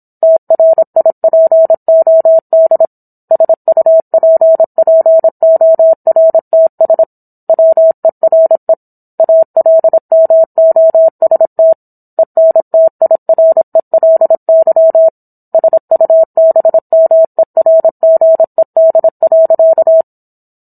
Morse code, heard as a keyed tone, about 27 wpm, 650 Hz. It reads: TRIPOD SUPPORTS WERE ALMOST ENTIRELY SUBMERGED.